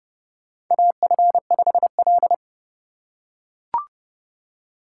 Morse code, heard as a keyed tone, about 30 wpm, 700 Hz. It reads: AF5L
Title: USA Call Signs - 4 Characters - 3x Character Spacing - 30wpm